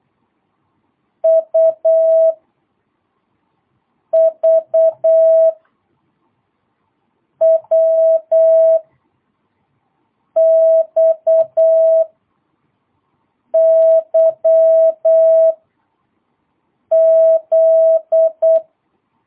Morse code, heard as a keyed tone, about 8 wpm, spaced as for 4 wpm, 650 Hz. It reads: U V W XYZ
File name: jh8cbh / 2022cw